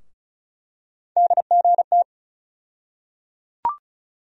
Morse code, heard as a keyed tone, about 34 wpm, 700 Hz. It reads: DGT
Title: USA Call Sign Postfixes - 3 Letters 35wpm